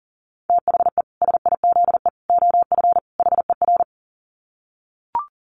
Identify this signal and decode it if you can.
Morse code, heard as a keyed tone, about 40 wpm, 700 Hz. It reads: THE SIZE OF HER
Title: Dit Challenge 40wpm